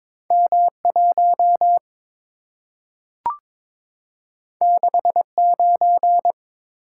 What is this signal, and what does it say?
Morse code, heard as a keyed tone, about 22 wpm, 700 Hz. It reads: M1 69
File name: practice / 22wpm / Two Letter-Number Combinations - 22wpm